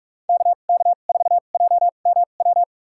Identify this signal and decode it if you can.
Morse code, heard as a keyed tone, about 46 wpm, spaced as for 32 wpm, 690 Hz. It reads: KK4JMW
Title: USA Call Signs - 6 Characters - 2x Character Spacing - 45wpm